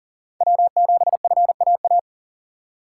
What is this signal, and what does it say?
Morse code, heard as a keyed tone, about 40 wpm, 700 Hz. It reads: W7FAA
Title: USA Call Signs - 5 Characters - 2x Character Spacing - 40wpm